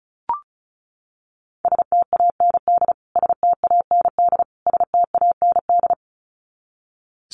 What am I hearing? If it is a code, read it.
Morse code, heard as a keyed tone, about 34 wpm, 700 Hz. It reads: STAND STAND STAND
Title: Top 500 Words Repeated 35wpm